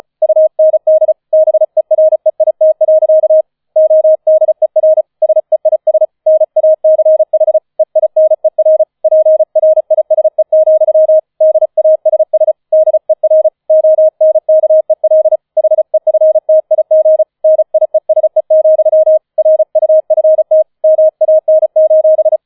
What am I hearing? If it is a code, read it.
Morse code, heard as a keyed tone, about 34 wpm, 630 Hz. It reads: UND BEREIT. ODER SEIS NACH EINER PRISE, DASS DER ONKEL HEFTIG NIESE, RUFT MAN:
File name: 005 Fünfter Streich 35wpm